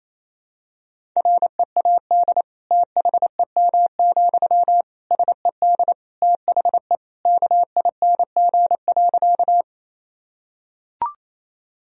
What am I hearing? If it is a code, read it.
Morse code, heard as a keyed tone, about 28 wpm, 700 Hz. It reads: READ THEM, SED THE KING.
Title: Sentences from Top 400 Words and Abbreviations 28wpm